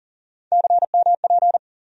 Morse code, heard as a keyed tone, about 40 wpm, 700 Hz. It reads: CMP
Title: USA Call Sign Postfixes - 3 Letters 40wpm